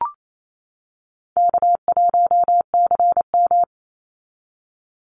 Morse code, heard as a keyed tone, about 28 wpm, 700 Hz. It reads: K1CM
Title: NA SOTA Chaser Callsigns 28wpm